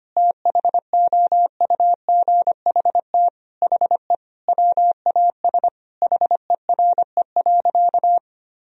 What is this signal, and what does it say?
Morse code, heard as a keyed tone, about 25 wpm, 700 Hz. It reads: THOUGHT HE WAS HERE.
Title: Sentences from Top 1000 Words - Review Part 3 - 25wpm